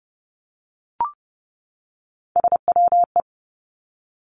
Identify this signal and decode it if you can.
Morse code, heard as a keyed tone, about 30 wpm, 700 Hz. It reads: SWE
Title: Three Challenging Letter-Number Combinations 30wpm